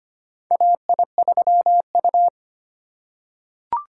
Morse code, heard as a keyed tone, about 25 wpm, 700 Hz. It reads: AI3U
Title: USA Call Signs - 8x Character Spacing - 25wpm